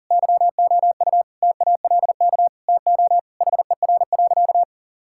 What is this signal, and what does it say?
Morse code, heard as a keyed tone, about 40 wpm, 700 Hz. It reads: YOU TALK TO HER.
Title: Sentences from Top 1000 Words - Review Part 6 - 40wpm